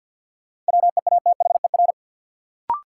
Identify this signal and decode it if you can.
Morse code, heard as a keyed tone, about 51 wpm, 700 Hz. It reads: WEATHER
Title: Sentences from Top 800 Words - New Word Drill - 50wpm